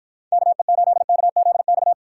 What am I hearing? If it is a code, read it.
Morse code, heard as a keyed tone, about 50 wpm, 700 Hz. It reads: KE8KZX